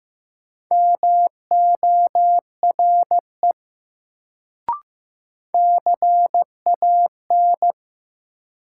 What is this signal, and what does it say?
Morse code, heard as a keyed tone, about 15 wpm, 700 Hz. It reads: MORE CAN
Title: Sets of 2 Words - New Word Drill - 15wpm